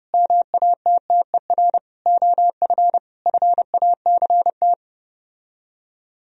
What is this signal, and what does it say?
Morse code, heard as a keyed tone, about 31 wpm, 700 Hz. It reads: MATTER OF FACT